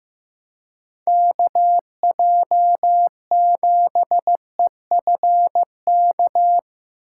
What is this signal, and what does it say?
Morse code, heard as a keyed tone, about 17 wpm, 700 Hz. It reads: KJ7EFK
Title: Speed Racing - USA 2x3 Call Signs 15wpm